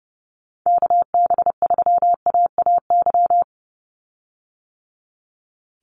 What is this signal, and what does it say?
Morse code, heard as a keyed tone, about 30 wpm, 700 Hz. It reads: KB3AAY
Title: Call Signs - 30wpm